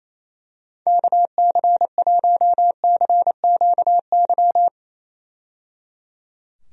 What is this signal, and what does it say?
Morse code, heard as a keyed tone, about 28 wpm, 700 Hz. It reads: KC1CQY
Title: USA Call Signs - 6 Characters - 28wpm